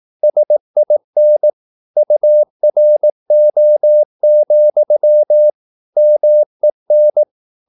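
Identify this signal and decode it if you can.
Morse code, heard as a keyed tone, about 18 wpm, 600 Hz. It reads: SIN URO, MEN
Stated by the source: Slow (QRS) Morse practice at 18 wpm, CW med Gnister.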